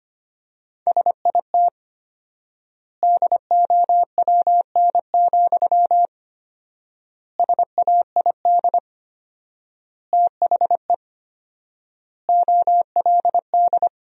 Morse code, heard as a keyed tone, about 25 wpm, 700 Hz. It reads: SIT DOWN, SAID THE OLD
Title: 4x Word-Spacing - Sentences from Top 500 Words - 25wpm